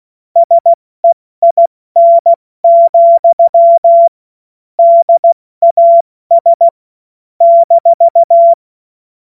Morse code, heard as a keyed tone, about 16 wpm, 680 Hz. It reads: SEIN, DAS -
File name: Witz-02-September-2025-70-F80